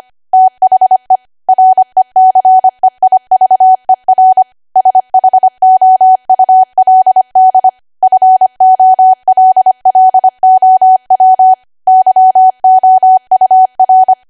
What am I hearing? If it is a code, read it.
Morse code, heard as a keyed tone, about 25 wpm, 750 Hz. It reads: THE RECEIVER SHOULD FOLLOW YOUR
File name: Archive / 25 WPM / 25WPM